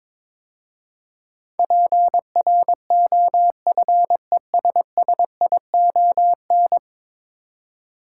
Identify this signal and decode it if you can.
Morse code, heard as a keyed tone, about 22 wpm, 700 Hz. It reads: PROFESSION
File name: practice / 22wpm / 10 Letter Words - First 3 Letters Given - 22wpm